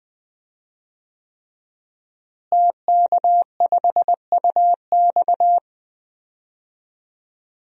Morse code, heard as a keyed tone, about 20 wpm, 700 Hz. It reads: TK5UX
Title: DXCC-100 Call Signs - 5 Characters - Repeated 3x - 20wpm